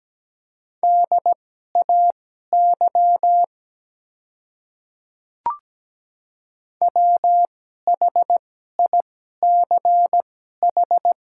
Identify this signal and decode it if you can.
Morse code, heard as a keyed tone, about 17 wpm, spaced as for 12 wpm, 700 Hz. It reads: DAY WHICH
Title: 2x Character Spacing - Top 100 Words - 17wpm